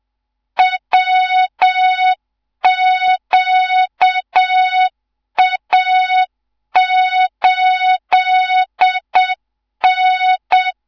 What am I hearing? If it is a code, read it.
Morse code, heard as a keyed tone, about 7 wpm, 3700 Hz, with noise. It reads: WQA8N
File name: Morze-HA2MN / hang / 14 280 HA2MN